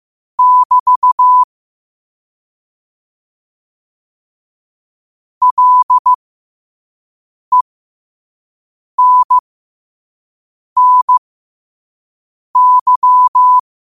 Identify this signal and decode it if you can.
Morse code, heard as a keyed tone, about 15 wpm, spaced as for 5 wpm, 1000 Hz. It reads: <BT> LENNY
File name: qotdfiles / 5wpm